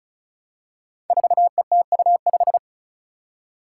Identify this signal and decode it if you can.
Morse code, heard as a keyed tone, about 36 wpm, 700 Hz. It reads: 4ETU5
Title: Five Challenging Letter-Number Combinations 35wpm